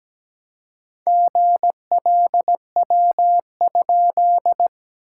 Morse code, heard as a keyed tone, about 17 wpm, 700 Hz. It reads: GLW?
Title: SOTA QSO Practice 17wpm